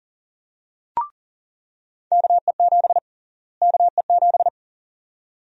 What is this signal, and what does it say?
Morse code, heard as a keyed tone, about 40 wpm, 700 Hz. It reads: KE7 KE7
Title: USA Call Sign Prefixes - Repeated 2x - 40wpm